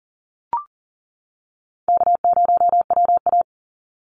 Morse code, heard as a keyed tone, about 40 wpm, 700 Hz. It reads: K0WA
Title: Easy Call Signs - 40wpm